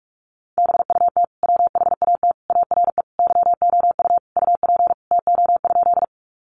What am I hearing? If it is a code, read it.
Morse code, heard as a keyed tone, about 44 wpm, 700 Hz. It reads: BUT WHAT ARE YOU UP TO?
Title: Sentences from Top 100 Words 45wpm